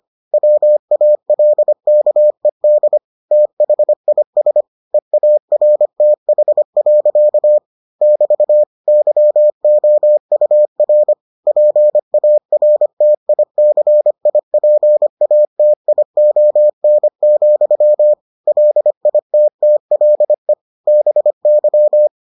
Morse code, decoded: WALKED THIS EARTH. <BT> YOUR PARTICIPATION, LITTLE BY

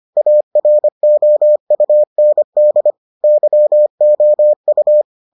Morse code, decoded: AROUND YOU

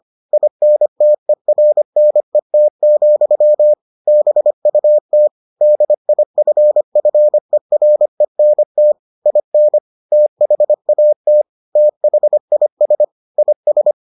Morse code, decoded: INTERNET, BUT DIFFERENT IN THAT THIS IS

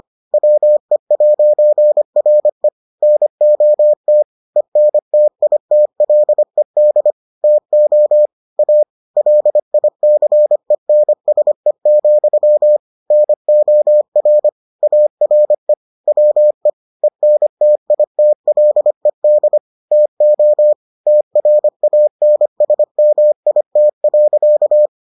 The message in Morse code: WE'RE NOT ENTITLED TO A LICENSE, NOR ARE WE ENTITLED TO TRANSMIT.

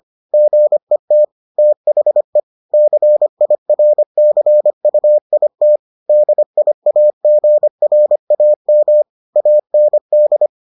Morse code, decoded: GET THE CIRCUIT DIAGRAM AND

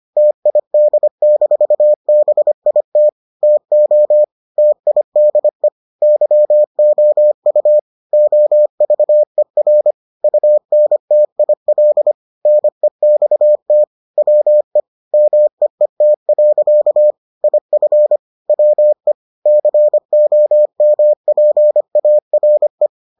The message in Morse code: TID-BIT TO TIDE YOU OVER UNTIL NEXT WE MEET. IF WE COMPARE

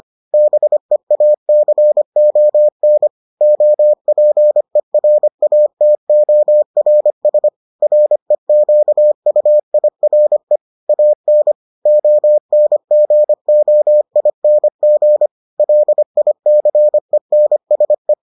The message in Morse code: BEACON OPERATORS REQUIRE AN ONGOING LICENSE

